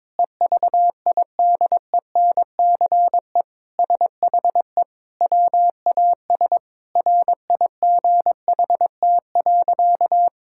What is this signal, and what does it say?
Morse code, heard as a keyed tone, about 22 wpm, 700 Hz. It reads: EVIDENCE SHE WAS RIGHT.
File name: practice / 22wpm / Sentences with 4-Word Phrases - 22wpm